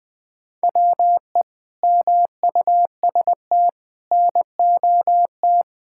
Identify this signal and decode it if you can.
Morse code, heard as a keyed tone, about 20 wpm, 700 Hz. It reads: WE MUST NOT
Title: Sentences from Top 1000 Words - Review Part 5 - 20wpm